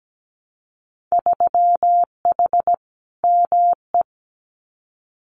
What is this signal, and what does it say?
Morse code, heard as a keyed tone, about 17 wpm, 700 Hz. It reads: 3H ME
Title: Winter Field Day Exchanges Encore 17wpm